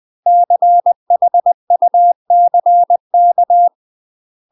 Morse code, heard as a keyed tone, about 20 wpm, 700 Hz. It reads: CHUCK
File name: UK MASTER QSOs TRAINING / 20WPM / UK MASTER QSOs 0006